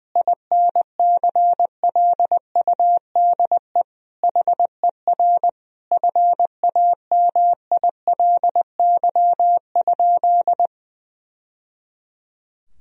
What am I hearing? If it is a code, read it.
Morse code, heard as a keyed tone, about 20 wpm, 700 Hz. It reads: INCLUDE HER FAMILY?
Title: Sentences from Top 1000 Words - Review Part 3 - 20wpm